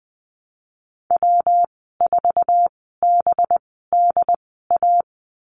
Morse code, heard as a keyed tone, about 20 wpm, spaced as for 15 wpm, 700 Hz. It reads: W4BDA
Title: 186 Lesson 30g - Letter b - Call Signs 20wpm